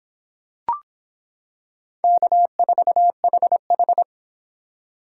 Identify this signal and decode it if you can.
Morse code, heard as a keyed tone, about 26 wpm, 700 Hz. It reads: K4HH